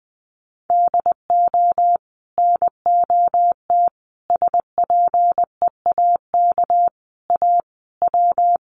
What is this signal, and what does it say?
Morse code, heard as a keyed tone, about 20 wpm, 700 Hz. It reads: DO NOT SPEAK A W
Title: Sentences from Top 1100 Words 20wpm